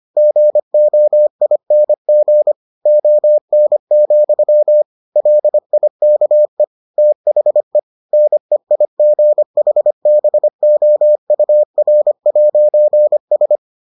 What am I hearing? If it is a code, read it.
Morse code, heard as a keyed tone, about 25 wpm, 600 Hz. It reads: GOING ON, LIKE THE NEIGHBOUR'S